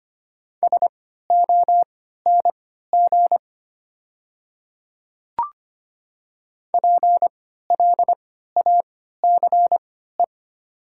Morse code, heard as a keyed tone, about 25 wpm, spaced as for 14 wpm, 700 Hz. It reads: SONG PLACE
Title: 3x Character Spacing - Top 500 Words - 25wpm